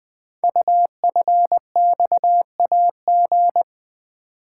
Morse code, heard as a keyed tone, about 20 wpm, 700 Hz. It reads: UFXAG
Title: Code Groups - 5 Letters - 20wpm